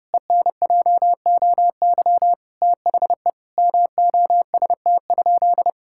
Morse code, heard as a keyed tone, about 30 wpm, 700 Hz. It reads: ENJOY THE MOST?
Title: Sentences from Top 1000 Words - Review Part 5 - 30wpm